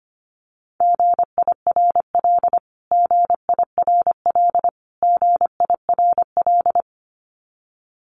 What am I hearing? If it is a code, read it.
Morse code, heard as a keyed tone, about 24 wpm, 700 Hz. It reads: GIRL GIRL GIRL
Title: Top 500 Words Repeated 25wpm